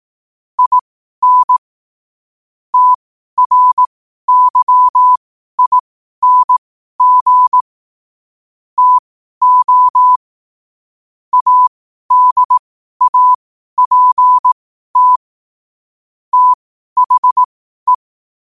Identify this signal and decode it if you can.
Morse code, heard as a keyed tone, about 18 wpm, spaced as for 12 wpm, 1000 Hz. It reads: IN TRYING TO ADAPT THE
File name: qotdfiles / 12wpm